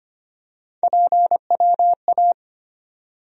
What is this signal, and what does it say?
Morse code, heard as a keyed tone, about 25 wpm, 700 Hz. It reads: PWA
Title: USA Call Sign Postfixes - 3 Letters 25wpm